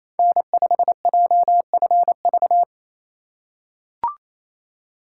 Morse code, heard as a keyed tone, about 28 wpm, 700 Hz. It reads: N5JFV
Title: USA Call Signs - 5 Characters - 4x Character Spacing - 28wpm